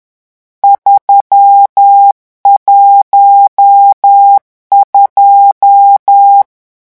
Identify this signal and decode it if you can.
Morse code, heard as a keyed tone, about 10 wpm, 800 Hz. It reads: 312
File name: morse mp3 / lecon10-vitesse 10